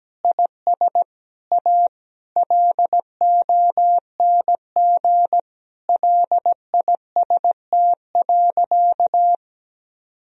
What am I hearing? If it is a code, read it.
Morse code, heard as a keyed tone, about 17 wpm, 700 Hz. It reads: IS A LONG LIST.
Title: Sentences from Top 1000 Words - Review Part 2 - 17wpm